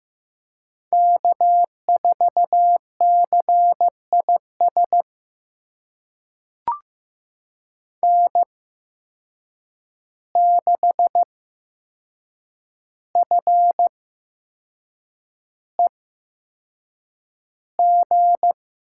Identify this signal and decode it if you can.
Morse code, heard as a keyed tone, about 15 wpm, spaced as for 6 wpm, 700 Hz. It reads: K4CIS N6FEG
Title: USA Call Signs - 5 Characters - 8x Character Spacing - 15wpm